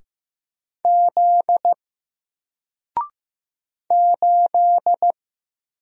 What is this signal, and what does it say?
Morse code, heard as a keyed tone, about 15 wpm, 700 Hz. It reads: Z 8
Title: Single Letter-Number Alt 15wpm